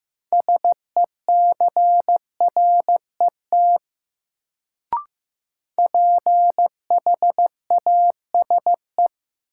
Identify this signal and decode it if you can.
Morse code, heard as a keyed tone, about 15 wpm, 700 Hz. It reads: SECRET PHASE
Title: Sentences from Top 1200 Words - New Word Drill - 15wpm